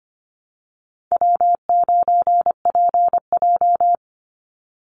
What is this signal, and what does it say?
Morse code, heard as a keyed tone, about 25 wpm, 700 Hz. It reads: W9PJ
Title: Easy Call Signs Repeated 2x - 25wpm